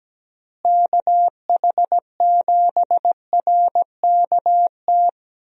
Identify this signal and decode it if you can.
Morse code, heard as a keyed tone, about 17 wpm, 700 Hz. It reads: KH7RKT
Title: USA Call Signs - 6 Characters - Repeated 3x - 17wpm